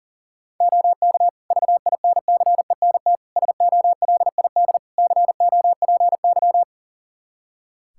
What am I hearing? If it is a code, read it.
Morse code, heard as a keyed tone, about 37 wpm, 700 Hz. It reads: OK VINCENT SOLID COPY
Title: Practice QSO - Exchange 3 - More Info - 40wpm